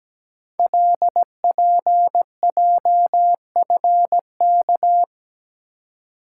Morse code, heard as a keyed tone, about 17 wpm, 700 Hz. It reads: LPJFK
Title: Code Groups - 5 Letters - 17wpm